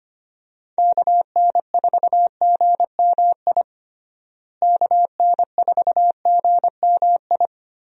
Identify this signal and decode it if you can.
Morse code, heard as a keyed tone, about 25 wpm, 700 Hz. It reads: KN4GMI KN4GMI
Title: USA Call Signs - 6 Characters - Repeated 3x - 25wpm